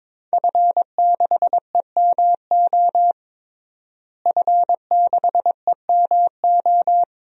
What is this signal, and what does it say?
Morse code, heard as a keyed tone, about 22 wpm, 700 Hz. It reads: F6EMO F6EMO
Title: DXCC-100 Call Signs - 5 Characters - Repeated 3x - 22wpm